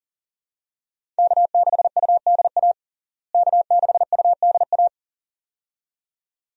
Morse code, heard as a keyed tone, about 40 wpm, 700 Hz. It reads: K6UDA K6UDA
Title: USA Call Signs - 5 Characters - Repeated 2x - 40wpm